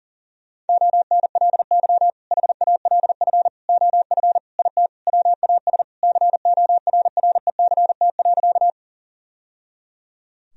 Morse code, heard as a keyed tone, about 40 wpm, 700 Hz. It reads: ONLY HALF OF IT WAS CORRECT.
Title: Sentences from Top 1000 Words - Review Part 2 - 40wpm